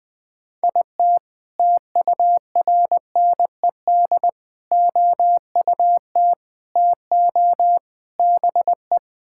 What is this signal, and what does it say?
Morse code, heard as a keyed tone, about 21 wpm, 700 Hz. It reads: IT TURNED OUT TO BE
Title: Build A Phrase - 5-Word Phrases - 20wpm